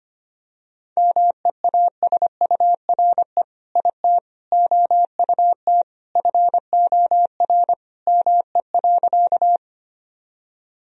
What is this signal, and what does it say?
Morse code, heard as a keyed tone, about 25 wpm, 700 Hz. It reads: MEASURE IT OUT FOR ME.
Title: Sentences from Top 1000 Words - Review Part 2 - 25wpm